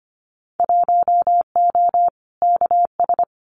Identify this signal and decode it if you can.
Morse code, heard as a keyed tone, about 25 wpm, 690 Hz. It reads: 1O KS
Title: Winter Field Day Exchanges 25wpm